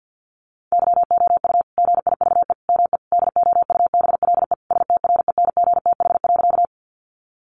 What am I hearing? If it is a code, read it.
Morse code, heard as a keyed tone, about 50 wpm, 700 Hz. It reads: YOU GIVE ME DOUBLE STRENGTH.
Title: Sentences from Top 1400 Words 50wpm